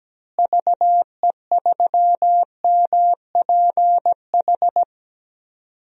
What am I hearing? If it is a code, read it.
Morse code, heard as a keyed tone, about 17 wpm, 700 Hz. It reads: VE3MPH
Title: DXCC-100 Call Signs - 6 Characters - Repeated 3x - 17wpm